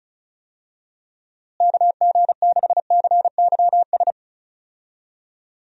Morse code, heard as a keyed tone, about 34 wpm, 700 Hz. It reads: KG6CYS
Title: USA Call Signs - 6 Characters - Repeated 2x - 35wpm